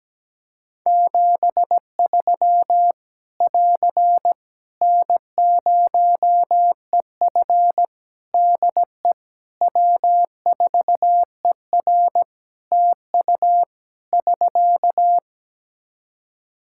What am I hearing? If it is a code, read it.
Morse code, heard as a keyed tone, about 17 wpm, 700 Hz. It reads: 73 <AR> N0EF DE W4ER TU <SK>
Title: Practice QSO - Exchange 4 - Ending - 17wpm